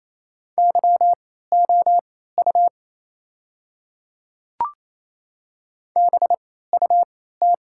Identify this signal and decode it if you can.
Morse code, heard as a keyed tone, about 27 wpm, spaced as for 16 wpm, 690 Hz. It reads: YOU BUT
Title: 3x Character Spacing - Top 100 Words - 28wpm